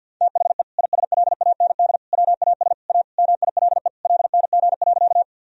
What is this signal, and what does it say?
Morse code, heard as a keyed tone, about 53 wpm, 700 Hz. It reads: THE ISLAND WAS A MILE LNG.
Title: Sentences from Top 500 Words and Abbreviations 50wpm